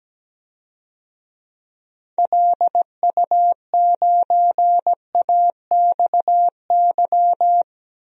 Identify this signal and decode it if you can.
Morse code, heard as a keyed tone, about 17 wpm, 700 Hz. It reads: LU9AXY
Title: DXCC-100 Call Signs - 6 Characters - 17wpm